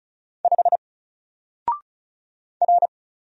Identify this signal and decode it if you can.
Morse code, heard as a keyed tone, about 34 wpm, 700 Hz. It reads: 5 R
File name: Single-Letter-Number-35wpm